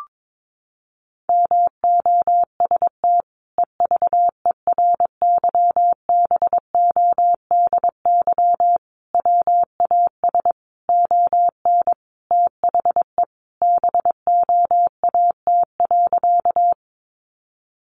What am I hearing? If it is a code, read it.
Morse code, heard as a keyed tone, about 22 wpm, 700 Hz. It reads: MOST EVERYBODY WAS ON THE BOAT.
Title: Sentences from Top 1400 Words 22wpm